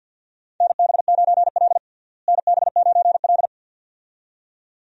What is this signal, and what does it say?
Morse code, heard as a keyed tone, about 52 wpm, 700 Hz. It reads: NB9L NB9L
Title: USA Call Signs - 4 Characters - Repeated 2x - 50wpm